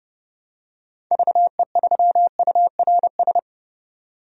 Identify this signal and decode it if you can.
Morse code, heard as a keyed tone, about 30 wpm, 700 Hz. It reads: VE3URS
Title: DXCC-100 Call Signs - 6 Characters - 30wpm